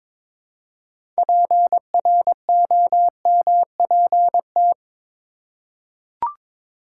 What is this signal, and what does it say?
Morse code, heard as a keyed tone, about 22 wpm, 700 Hz. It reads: PROMPT
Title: Six Letter Words - 22wpm